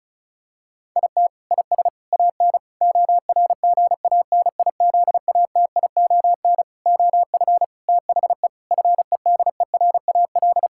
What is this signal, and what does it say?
Morse code, heard as a keyed tone, about 36 wpm, 700 Hz. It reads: IT IS AN ORGANIZATION OF THE FEDERAL
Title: Sentences from Top 1000 Words - Review Part 6 - 35wpm